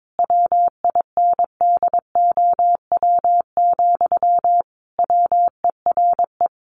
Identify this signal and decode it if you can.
Morse code, heard as a keyed tone, about 22 wpm, 700 Hz. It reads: WINDOW, WERE
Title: A Sound Of Thunder 22wpm